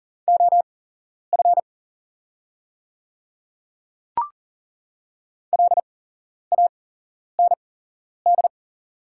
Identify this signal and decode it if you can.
Morse code, heard as a keyed tone, about 40 wpm, spaced as for 11 wpm, 700 Hz. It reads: OF LAND
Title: 8x Character Spacing - Top 300 Words - 40wpm